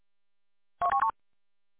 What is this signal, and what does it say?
There is no code.